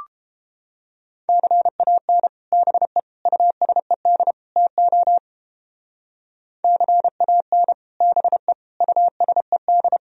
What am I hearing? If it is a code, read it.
Morse code, heard as a keyed tone, about 30 wpm, 700 Hz. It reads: CAN BE USED TO CAN BE USED